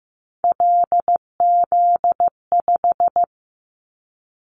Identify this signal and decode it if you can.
Morse code, heard as a keyed tone, about 15 wpm, 700 Hz. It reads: LZ5
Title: Callsign Prefixes for Top 100 DXCC - 15wpm